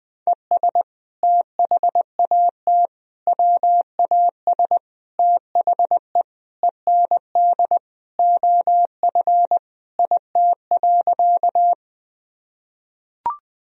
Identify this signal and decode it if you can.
Morse code, heard as a keyed tone, about 20 wpm, 700 Hz. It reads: ES THAT WAS THE END OF IT.
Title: Sentences from Top 200 Words and Abbreviations 20wpm